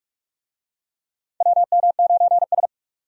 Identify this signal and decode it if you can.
Morse code, heard as a keyed tone, about 46 wpm, 700 Hz. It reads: WM9S
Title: USA Call Signs - 4 Characters - 45wpm